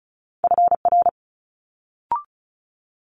Morse code, heard as a keyed tone, about 36 wpm, 700 Hz. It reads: FR
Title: Multi-letter Phonemes Quick Fire 35wpm